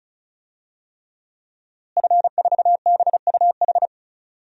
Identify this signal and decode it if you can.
Morse code, heard as a keyed tone, about 36 wpm, 700 Hz. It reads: F4BUH
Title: DXCC-100 Call Signs - 5 Characters - Repeated 3x - 35wpm